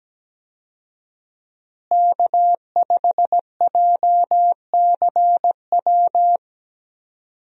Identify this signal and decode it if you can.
Morse code, heard as a keyed tone, about 17 wpm, 700 Hz. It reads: K5JCW
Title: USA Call Signs - 5 Characters - Repeated 3x - 17wpm